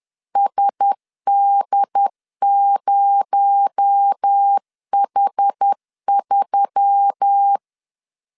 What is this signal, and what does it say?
Morse code, heard as a keyed tone, about 10 wpm, 800 Hz. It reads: SD0H3
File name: LEZIONI / 7 GF9Z / 7 GF9 10wpm 2747